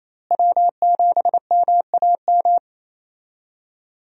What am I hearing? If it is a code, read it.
Morse code, heard as a keyed tone, about 28 wpm, 700 Hz. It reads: W7MAM
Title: USA Call Signs - 5 Characters - Repeated 3x - 28wpm